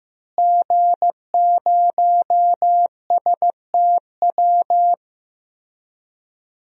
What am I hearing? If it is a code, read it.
Morse code, heard as a keyed tone, about 15 wpm, 700 Hz. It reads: G0STW
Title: DXCC-100 Call Signs - 5 Characters - Repeated 3x - 15wpm